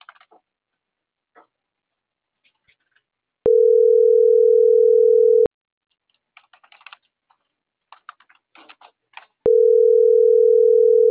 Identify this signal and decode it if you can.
Ringback tone (North America), 440 + 480 Hz.